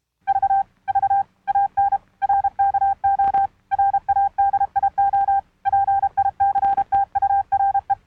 Morse code, heard as a keyed tone, about 32 wpm, 790 Hz. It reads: V V AN RKO RADIO PICTURE